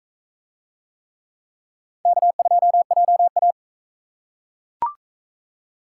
Morse code, heard as a keyed tone, about 42 wpm, 700 Hz. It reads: K2JA